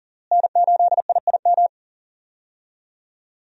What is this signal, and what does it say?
Morse code, heard as a keyed tone, about 40 wpm, 700 Hz. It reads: N8IIM
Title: USA Call Signs - 5 Characters - Repeated 2x - 40wpm